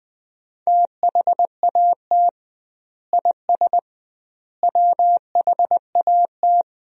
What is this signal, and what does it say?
Morse code, heard as a keyed tone, about 20 wpm, 700 Hz. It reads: THAT IS WHAT
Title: 2x Word-Spacing - Sentences from Top 100 Words - 20wpm